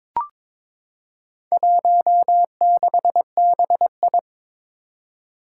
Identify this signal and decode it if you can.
Morse code, heard as a keyed tone, about 21 wpm, 700 Hz. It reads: 16BI
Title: Four Challenging Letter-Number Combinations 22wpm